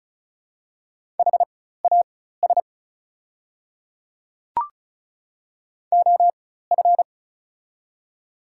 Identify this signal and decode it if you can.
Morse code, heard as a keyed tone, about 34 wpm, spaced as for 16 wpm, 700 Hz. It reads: HAS OF
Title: 4x Character Spacing - Top 300 Words - 35wpm